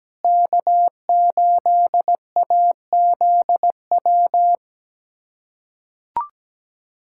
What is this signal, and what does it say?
Morse code, heard as a keyed tone, about 17 wpm, 700 Hz. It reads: K8AZW
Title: USA Callsign DE Callsign - 5 Characters - 17wpm